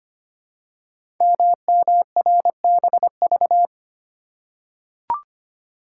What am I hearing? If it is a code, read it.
Morse code, heard as a keyed tone, about 25 wpm, 700 Hz. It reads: MMRBV
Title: Code Groups - 5 Letters - 25wpm